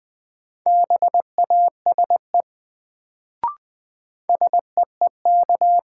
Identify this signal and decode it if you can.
Morse code, heard as a keyed tone, about 20 wpm, 700 Hz. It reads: BASE SEEK
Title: Sentences from Top 1600 Words - New Word Drill Alt - 20wpm